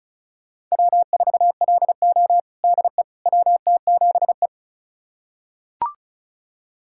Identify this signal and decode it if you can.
Morse code, heard as a keyed tone, about 36 wpm, 700 Hz. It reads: W4LO DE WT7E
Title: USA Callsign DE Callsign - 4 Characters - 35wpm